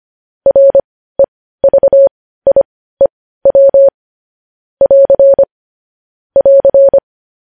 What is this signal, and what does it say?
Morse code, heard as a keyed tone, about 25 wpm, spaced as for 15 wpm, 550 Hz. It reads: REVIEW <AR> <AR>
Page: news-headlines-in-morse-code-at-15-wpm